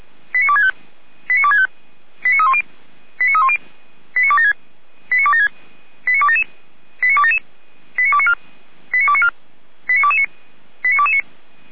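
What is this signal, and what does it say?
Five-tone selective call (ZVEI), heard as quick runs of five tones, tones of about 70 ms. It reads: 89265 89265 89310 89310 89276 89276 89277 89277 89294 89294 89229 89229
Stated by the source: ZVEI five-tone call 89067 received by SDR, fire brigade Feuerwehr Orlamünde.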